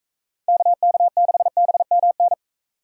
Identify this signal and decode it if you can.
Morse code, heard as a keyed tone, about 39 wpm, 700 Hz. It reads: KK6BMN